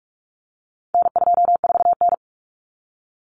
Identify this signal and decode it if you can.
Morse code, heard as a keyed tone, about 44 wpm, 700 Hz. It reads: N24N
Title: Common Cut Numbers - 4-Digit 45wpm